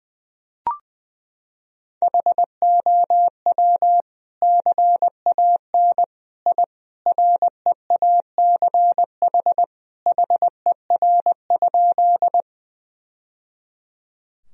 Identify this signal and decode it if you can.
Morse code, heard as a keyed tone, about 21 wpm, 700 Hz. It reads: HOW CAN I REACH HER?
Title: Sentences from Top 1000 Words - Review Part 2 - 20wpm